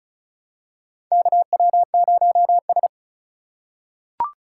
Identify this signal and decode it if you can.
Morse code, heard as a keyed tone, about 36 wpm, 700 Hz. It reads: KW0S
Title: USA Call Signs - 4 Characters - 2x Character Spacing - 35wpm